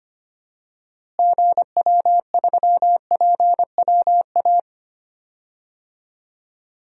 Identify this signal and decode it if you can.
Morse code, heard as a keyed tone, about 25 wpm, 700 Hz. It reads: GW3PWA
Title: DXCC-100 Call Signs - 6 Characters - Repeated 3x - 25wpm